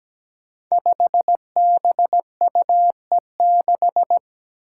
Morse code, heard as a keyed tone, about 17 wpm, 700 Hz. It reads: 5BUE6
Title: Five Challenging Letter-Number Combinations 17wpm